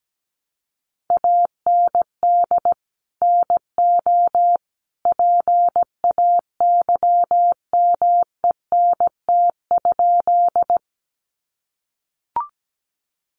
Morse code, heard as a keyed tone, about 17 wpm, 700 Hz. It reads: AND NO PAYMENT?
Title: Sentences from Top 1200 Words 17wpm